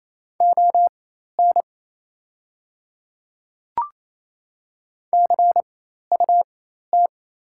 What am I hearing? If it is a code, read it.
Morse code, heard as a keyed tone, about 28 wpm, spaced as for 13 wpm, 700 Hz. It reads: ON CUT
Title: 4x Character Spacing - Top 500 Words - 28wpm